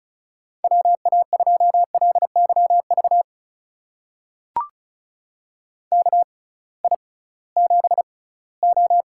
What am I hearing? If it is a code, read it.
Morse code, heard as a keyed tone, about 36 wpm, spaced as for 20 wpm, 700 Hz. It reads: WA2LYV KI7O
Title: USA Call Signs - 6 Characters - 6x Character Spacing - 35wpm